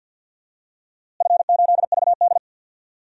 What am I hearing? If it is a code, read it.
Morse code, heard as a keyed tone, about 46 wpm, 700 Hz. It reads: F8VD
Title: DXCC-100 Call Signs - 4 Characters - Repeated 3x - 50wpm